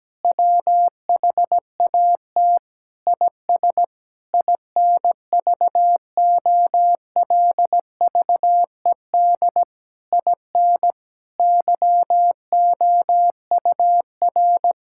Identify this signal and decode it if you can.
Morse code, heard as a keyed tone, about 17 wpm, 700 Hz. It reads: WHAT IS INVOLVED IN YOUR
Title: Sentences from Top 1000 Words - Review Part 5 - 17wpm